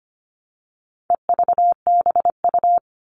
Morse code, heard as a keyed tone, about 25 wpm, 700 Hz. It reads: EVBU
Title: All Cut Numbers - 4-Digit 25wpm